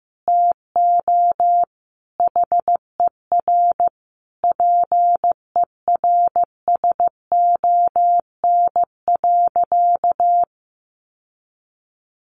Morse code, heard as a keyed tone, about 15 wpm, 700 Hz. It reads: TO HER PERSON.
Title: Sentences from Top 1400 Words 15wpm